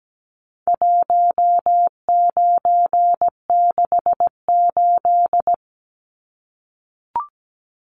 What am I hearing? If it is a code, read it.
Morse code, heard as a keyed tone, about 17 wpm, 700 Hz. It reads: 1968
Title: Numbers - 4-Digit 17wpm